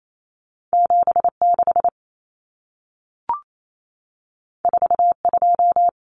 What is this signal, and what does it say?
Morse code, heard as a keyed tone, about 28 wpm, 700 Hz. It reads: 76 42
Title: Numbers - 2-Digit 28wpm